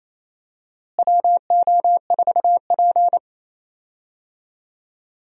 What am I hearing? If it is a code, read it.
Morse code, heard as a keyed tone, about 28 wpm, 700 Hz. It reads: WO4P
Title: USA Call Signs - 4 Characters - Repeated 3x - 28wpm